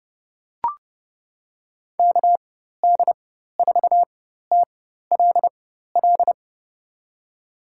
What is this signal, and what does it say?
Morse code, heard as a keyed tone, about 30 wpm, spaced as for 14 wpm, 700 Hz. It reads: KD4TLL
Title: USA Call Signs - 6 Characters - 4x Character Spacing - 30wpm